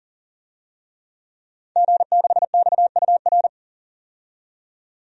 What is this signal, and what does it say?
Morse code, heard as a keyed tone, about 37 wpm, 700 Hz. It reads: G6XUR
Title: DXCC-100 Call Signs - 5 Characters - 40wpm